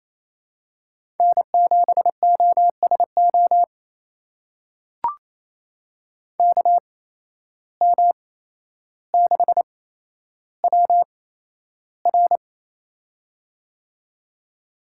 Morse code, heard as a keyed tone, about 28 wpm, spaced as for 12 wpm, 700 Hz. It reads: N7OSO KM6WR
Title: USA Call Signs - 5 Characters - 8x Character Spacing - 28wpm